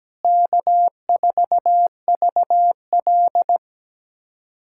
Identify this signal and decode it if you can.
Morse code, heard as a keyed tone, about 17 wpm, 700 Hz. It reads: K4VL
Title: USA Callsign DE Callsign - 4 Characters - 17wpm